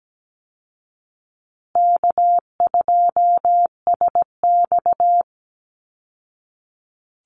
Morse code, heard as a keyed tone, about 17 wpm, 700 Hz. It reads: K2SX
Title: Call Signs Repeated 3x - 17wpm